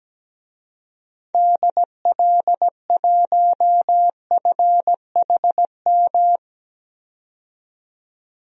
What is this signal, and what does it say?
Morse code, heard as a keyed tone, about 17 wpm, 700 Hz. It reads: DL1FHM
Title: DXCC-100 Call Signs - 6 Characters - 17wpm